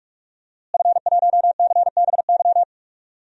Morse code, heard as a keyed tone, about 47 wpm, 700 Hz. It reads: F1CBY
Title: DXCC-100 Call Signs - 5 Characters - Repeated 3x - 45wpm